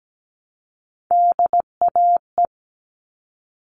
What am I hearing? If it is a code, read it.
Morse code, heard as a keyed tone, about 17 wpm, 700 Hz. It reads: DAE
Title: All Cut Numbers - 3-Digit 17wpm